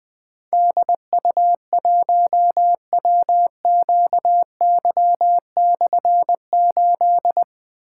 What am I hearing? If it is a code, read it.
Morse code, heard as a keyed tone, about 20 wpm, 700 Hz. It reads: DU1WQY/8
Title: Difficult Call Signs 20wpm